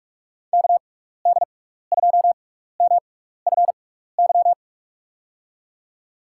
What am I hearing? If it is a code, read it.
Morse code, heard as a keyed tone, about 46 wpm, spaced as for 15 wpm, 700 Hz. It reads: KD2MFY
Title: USA Call Signs - 6 Characters - 6x Character Spacing - 45wpm